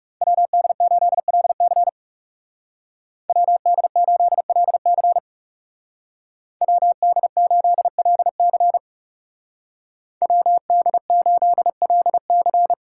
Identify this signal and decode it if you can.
Morse code, heard as a keyed tone, about 39 wpm, 700 Hz. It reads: WD8LC WD8LC WD8LC WD8LC